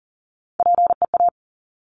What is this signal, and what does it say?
Morse code, heard as a keyed tone, about 41 wpm, 700 Hz. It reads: PEA
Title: Top Three Letter Combinations in Words 40wpm